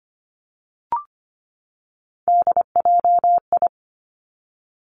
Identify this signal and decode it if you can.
Morse code, heard as a keyed tone, about 25 wpm, 700 Hz. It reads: DJI
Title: Three Challenging Letter-Number Combinations 25wpm